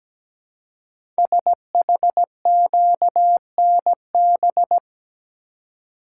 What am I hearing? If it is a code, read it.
Morse code, heard as a keyed tone, about 17 wpm, 700 Hz. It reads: SHQNB
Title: Code Groups - 5 Letters - 17wpm